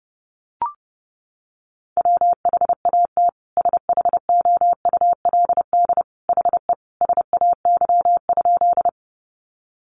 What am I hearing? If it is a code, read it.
Morse code, heard as a keyed tone, about 30 wpm, 700 Hz. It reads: WHAT SHOULD HE SAY?
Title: Sentences from Top 300 Words 30wpm